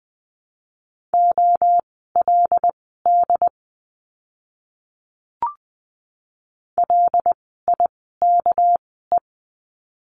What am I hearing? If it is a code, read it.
Morse code, heard as a keyed tone, about 20 wpm, spaced as for 15 wpm, 700 Hz. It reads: OLD LIKE
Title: 244 Lesson 39f - Letter z - Words 20wpm